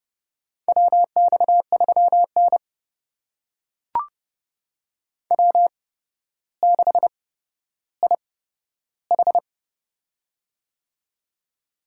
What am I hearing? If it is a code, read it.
Morse code, heard as a keyed tone, about 30 wpm, spaced as for 13 wpm, 700 Hz. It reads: WX3N W6IH
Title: USA Call Signs - 4 Characters - 8x Character Spacing - 30wpm